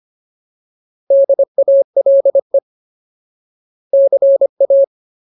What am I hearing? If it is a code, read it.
Morse code, heard as a keyed tone, about 25 wpm, 550 Hz. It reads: DALE CA
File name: Ultimate Word List / 25WPM Ultimate